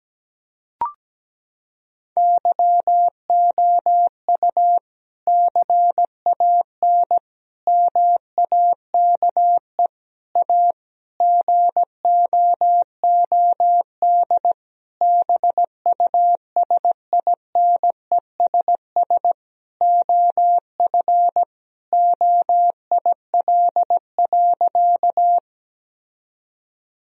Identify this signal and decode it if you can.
Morse code, heard as a keyed tone, about 17 wpm, 700 Hz. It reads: YOU CAN MAKE A GOOD BUSINESS OF OIL.
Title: Sentences from Top 1000 Words - Review Part 4 - 17wpm